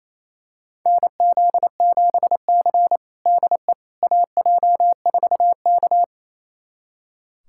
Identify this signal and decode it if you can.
Morse code, heard as a keyed tone, about 28 wpm, 700 Hz. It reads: NZ7C DE AJ4K
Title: USA Callsign DE Callsign - 4 Characters - 28wpm